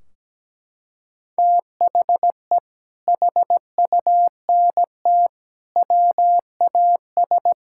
Morse code, heard as a keyed tone, about 17 wpm, 700 Hz. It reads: THE HUNT WAS
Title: Sentences from Top 1600 Words 17wpm